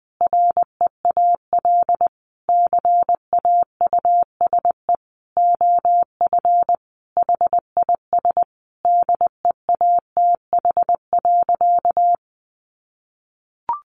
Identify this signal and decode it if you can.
Morse code, heard as a keyed tone, about 20 wpm, 700 Hz. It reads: REAL CAUSE OF HIS DEATH.
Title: Sentences from Top 1100 Words 20wpm